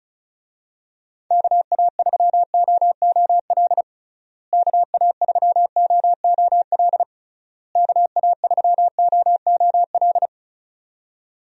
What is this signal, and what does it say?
Morse code, heard as a keyed tone, about 36 wpm, 700 Hz. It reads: KA3OOL KA3OOL KA3OOL
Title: USA Call Signs - 6 Characters - Repeated 3x - 35wpm